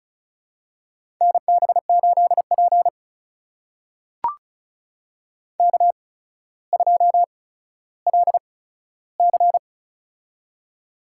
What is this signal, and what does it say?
Morse code, heard as a keyed tone, about 36 wpm, spaced as for 15 wpm, 700 Hz. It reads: NB8P K2LC
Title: USA Call Signs - 4 Characters - 8x Character Spacing - 35wpm